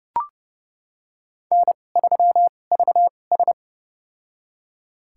Morse code, heard as a keyed tone, about 28 wpm, spaced as for 21 wpm, 690 Hz. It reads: N3VS
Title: USA Call Signs - 4 Characters - 2x Character Spacing - 30wpm